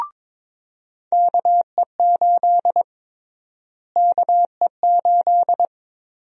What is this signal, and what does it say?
Morse code, heard as a keyed tone, about 22 wpm, 700 Hz. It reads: KE8 KE8
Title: USA Call Sign Prefixes - Repeated 2x - 22wpm